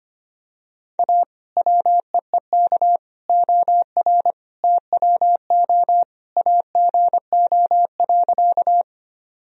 Morse code, heard as a keyed tone, about 25 wpm, 700 Hz. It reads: A WEEK OR TWO AGO.
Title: Sentences from Top 1000 Words - Review Part 3 - 25wpm